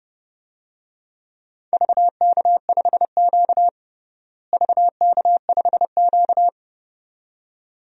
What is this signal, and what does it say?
Morse code, heard as a keyed tone, about 32 wpm, 700 Hz. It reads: VK5Q VK5Q
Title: DXCC-100 Call Signs - 4 Characters - Repeated 2x - 30wpm